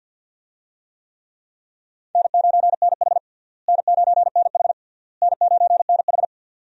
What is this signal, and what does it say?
Morse code, heard as a keyed tone, about 47 wpm, 700 Hz. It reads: N9NH N9NH N9NH
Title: USA Call Signs - 4 Characters - Repeated 3x - 50wpm